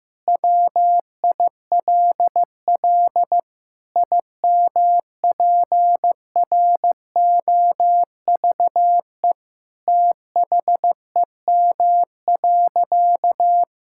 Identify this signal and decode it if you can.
Morse code, heard as a keyed tone, about 15 wpm, 700 Hz. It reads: WILL IMPROVE THEM.
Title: Sentences from Top 1000 Words - Review Part 5 - 15wpm